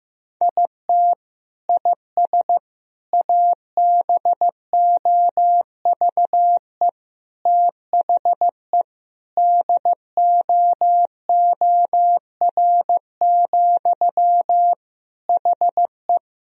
Morse code, heard as a keyed tone, about 15 wpm, 700 Hz. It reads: IT IS ABOVE THE DOOR, HE
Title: Sentences from Top 1000 Words - Review Part 2 - 15wpm